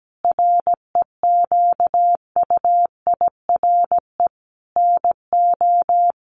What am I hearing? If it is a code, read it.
Morse code, heard as a keyed tone, about 17 wpm, 700 Hz. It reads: REQUIRE NO
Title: Sentences from Top 1500 Words 17wpm